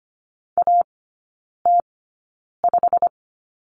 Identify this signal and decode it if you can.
Morse code, heard as a keyed tone, about 25 wpm, spaced as for 9 wpm, 700 Hz. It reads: AT5
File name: Lesson 17e - Letter u - Three Characters 25wpm